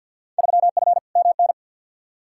Morse code, heard as a keyed tone, about 53 wpm, 700 Hz. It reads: 3F MN
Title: Field Day Exchanges Encore - 50wpm